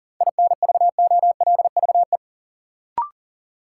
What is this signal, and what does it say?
Morse code, heard as a keyed tone, about 39 wpm, 700 Hz. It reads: INVOLVE
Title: Sentences from Top 1200 Words - New Word Drill Alt - 40wpm